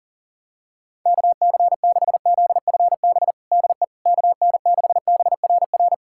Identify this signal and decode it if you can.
Morse code, heard as a keyed tone, about 37 wpm, 700 Hz. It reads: KC6ZFB DE KN6BRR
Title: USA Callsign DE Callsign 40wpm